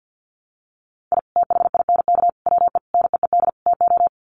Morse code, heard as a keyed tone, about 49 wpm, 700 Hz. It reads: I THINK WE NEED TO
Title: Sets of 5 Words 50wpm